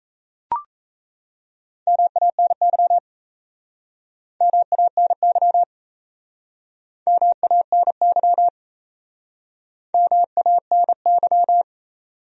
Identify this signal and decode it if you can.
Morse code, heard as a keyed tone, about 40 wpm, 700 Hz. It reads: MANY MANY MANY MANY